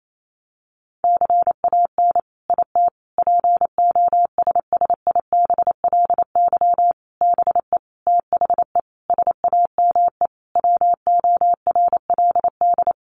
Morse code, heard as a keyed tone, about 28 wpm, 700 Hz. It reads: CAN IT POSSIBLY BE THE SAME WORLD
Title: Sentences from Top 1300 Words 28wpm